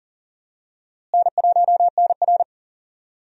Morse code, heard as a keyed tone, about 37 wpm, 700 Hz. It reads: N1NR
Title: USA Call Signs - 4 Characters - Repeated 2x - 40wpm